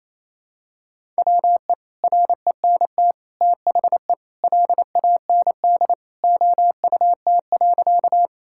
Morse code, heard as a keyed tone, about 28 wpm, 700 Hz. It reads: WE RENT THE LAND OUT.
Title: Sentences from Top 1000 Words - Review Part 5 - 28wpm